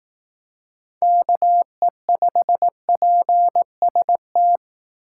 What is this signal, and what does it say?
Morse code, heard as a keyed tone, about 18 wpm, 700 Hz. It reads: KE5PST
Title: Speed Racing - USA Call Signs 15wpm